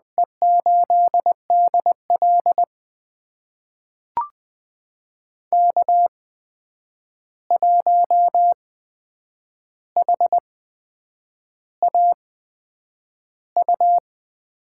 Morse code, heard as a keyed tone, about 20 wpm, spaced as for 8 wpm, 700 Hz. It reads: E8DL K1HAU
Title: USA Call Signs - 8x Character Spacing - 20wpm